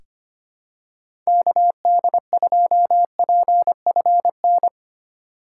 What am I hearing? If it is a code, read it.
Morse code, heard as a keyed tone, about 25 wpm, 700 Hz. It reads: KD2PFN
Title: USA Call Signs - 6 Characters - 2x Character Spacing - 25wpm